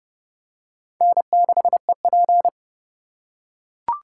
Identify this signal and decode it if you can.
Morse code, heard as a keyed tone, about 30 wpm, 700 Hz. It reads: N6EP
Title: USA Call Signs - 3x Character Spacing - 30wpm